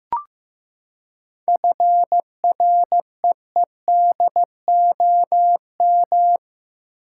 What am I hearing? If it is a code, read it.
Morse code, heard as a keyed tone, about 15 wpm, 700 Hz. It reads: FREEDOM
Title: Seven Letter Words - 15wpm